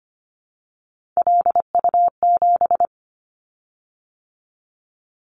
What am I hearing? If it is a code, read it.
Morse code, heard as a keyed tone, about 25 wpm, 700 Hz. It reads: LU7
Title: Callsign Prefixes for Top 100 DXCC - 25wpm